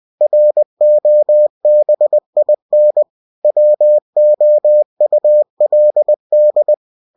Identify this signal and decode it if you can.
Morse code, heard as a keyed tone, about 20 wpm, 600 Hz. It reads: ROBIN WOULD
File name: cw-books / winnie-the-pooh / winnie-20wpm-page-0071